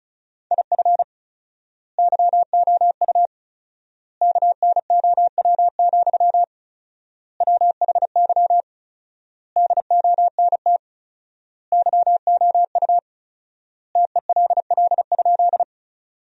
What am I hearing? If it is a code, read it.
Morse code, heard as a keyed tone, about 34 wpm, 700 Hz. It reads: IF YOU KNOW, WHY DONT YOU TELL?
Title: 4x Word-Spacing - Sentences from Top 300 Words - 35wpm